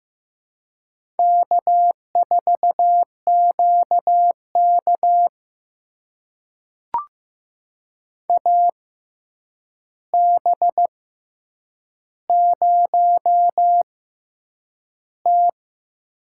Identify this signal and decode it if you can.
Morse code, heard as a keyed tone, about 15 wpm, spaced as for 8 wpm, 700 Hz. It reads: K4QK AB0T
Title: USA Call Signs - 4 Characters - 6x Character Spacing - 15wpm